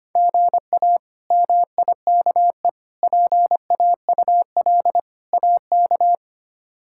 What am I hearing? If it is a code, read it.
Morse code, heard as a keyed tone, about 25 wpm, 700 Hz. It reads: GA MIKE PAUL AK
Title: K1USN SST QSO Practice - 25wpm